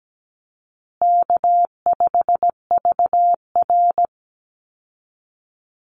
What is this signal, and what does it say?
Morse code, heard as a keyed tone, about 17 wpm, 700 Hz. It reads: K5VR
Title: Call Signs Repeated 2x - 17wpm